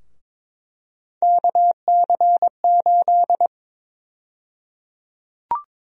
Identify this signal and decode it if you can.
Morse code, heard as a keyed tone, about 22 wpm, 700 Hz. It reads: KC8
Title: USA Call Sign Prefixes - Repeated 2x - 22wpm